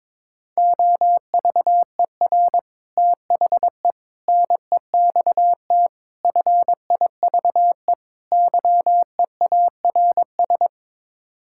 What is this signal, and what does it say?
Morse code, heard as a keyed tone, about 22 wpm, 700 Hz. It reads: OVER THE NEXT FIVE YEARS
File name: practice / 22wpm / Build A Phrase - 5-Word Phrases - 22wpm